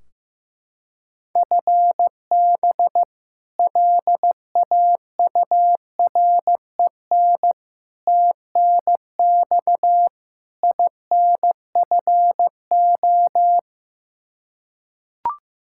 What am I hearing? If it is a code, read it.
Morse code, heard as a keyed tone, about 15 wpm, 700 Hz. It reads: FB LAUREN TNX INFO
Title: K1USN SST QSO Practice 15wpm